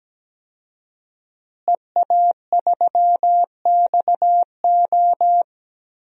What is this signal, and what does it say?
Morse code, heard as a keyed tone, about 17 wpm, 700 Hz. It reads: EA3XO
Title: DXCC-100 Call Signs - 5 Characters - Repeated 3x - 17wpm